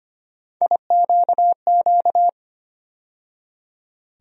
Morse code, heard as a keyed tone, about 25 wpm, 700 Hz. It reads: IQQ
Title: USA Call Sign Postfixes - 3 Letters 25wpm